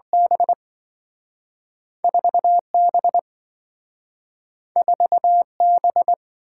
Morse code, heard as a keyed tone, about 27 wpm, spaced as for 5 wpm, 700 Hz. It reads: B4 B4 B